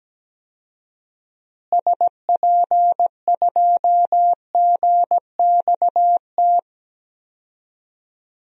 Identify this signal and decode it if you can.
Morse code, heard as a keyed tone, about 17 wpm, 700 Hz. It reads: SP2GXT
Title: DXCC-100 Call Signs - 6 Characters - Repeated 2x - 17wpm